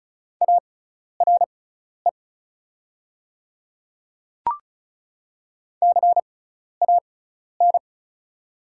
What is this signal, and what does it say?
Morse code, heard as a keyed tone, about 36 wpm, spaced as for 12 wpm, 700 Hz. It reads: ARE CAN
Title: 6x Character Spacing - Top 200 Words - 35wpm